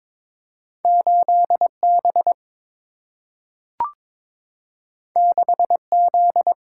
Morse code, heard as a keyed tone, about 22 wpm, 700 Hz. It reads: 8 B 6 Z
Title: Single-Letter Single-Number 22wpm